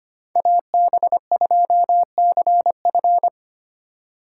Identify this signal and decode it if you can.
Morse code, heard as a keyed tone, about 25 wpm, 700 Hz. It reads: AB2CF
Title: USA Call Signs - 5 Characters - 2x Character Spacing - 25wpm